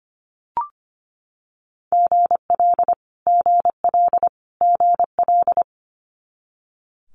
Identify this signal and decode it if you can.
Morse code, heard as a keyed tone, about 25 wpm, 700 Hz. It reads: GL GL GL
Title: Multi-letter Phonemes Repeated 3x 25wpm